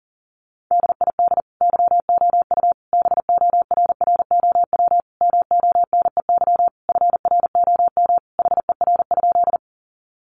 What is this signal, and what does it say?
Morse code, heard as a keyed tone, about 40 wpm, 700 Hz. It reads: DID YOU BORROW MONEY FROM HER?
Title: Sentences from Top 1500 Words 40wpm